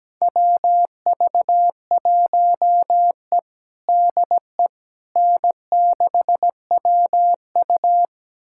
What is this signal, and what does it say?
Morse code, heard as a keyed tone, about 17 wpm, 700 Hz. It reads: WV1E DE N6WU
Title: USA Callsign DE Callsign - 4 Characters - 17wpm